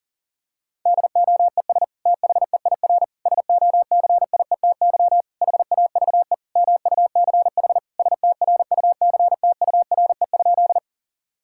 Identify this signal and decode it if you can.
Morse code, heard as a keyed tone, about 40 wpm, 690 Hz. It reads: DOES THEIR SOCIETY HAVE MUCH STRUCTURE?
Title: Sentences from Top 1000 Words - Review Part 6 - 40wpm